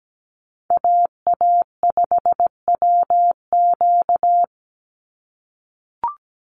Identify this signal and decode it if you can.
Morse code, heard as a keyed tone, about 17 wpm, 700 Hz. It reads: AA5WQ
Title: USA Call Signs 17wpm